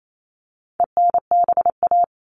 Morse code, heard as a keyed tone, about 28 wpm, 700 Hz. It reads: ENBA
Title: All Cut Numbers - 4-Digit 28wpm